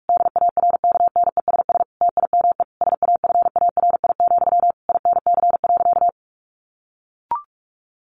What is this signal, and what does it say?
Morse code, heard as a keyed tone, about 44 wpm, 700 Hz. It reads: DARKNESS TIME SAFARI, INC.
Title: A Sound Of Thunder 45wpm